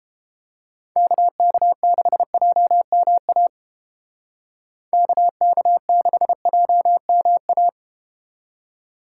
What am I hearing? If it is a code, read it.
Morse code, heard as a keyed tone, about 31 wpm, 700 Hz. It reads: KK6JMA KK6JMA